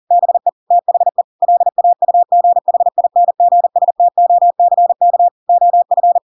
Morse code, heard as a keyed tone, about 40 wpm, 700 Hz. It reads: BE THE LAUGHINGSTOCK OF
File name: Pinocchio / Pinocchio ch4 40 40 ik1hge